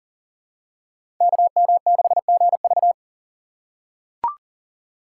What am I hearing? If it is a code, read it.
Morse code, heard as a keyed tone, about 42 wpm, 700 Hz. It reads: KM6GV
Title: USA Call Signs - 5 Characters - 6x Character Spacing - 40wpm